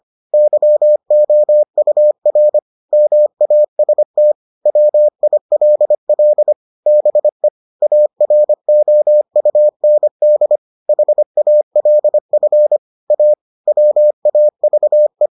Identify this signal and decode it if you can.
Morse code, heard as a keyed tone, about 25 wpm, 600 Hz. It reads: YOUR MAST WILL BE AROUND HALF A WAVE